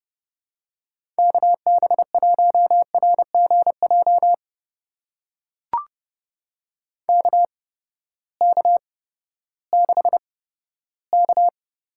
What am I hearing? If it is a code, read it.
Morse code, heard as a keyed tone, about 31 wpm, spaced as for 15 wpm, 700 Hz. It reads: KB1RGJ KK6K
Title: USA Call Signs - 6 Characters - 8x Character Spacing - 30wpm